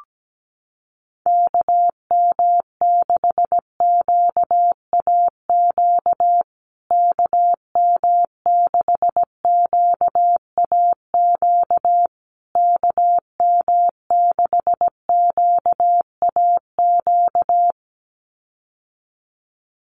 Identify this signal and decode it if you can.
Morse code, heard as a keyed tone, about 17 wpm, 700 Hz. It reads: KM6QAQ KM6QAQ KM6QAQ
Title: USA Call Signs Repeated 3x 17wpm